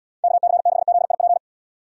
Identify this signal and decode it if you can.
Morse code, heard as a keyed tone, about 78 wpm, 700 Hz. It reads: KC6ZIP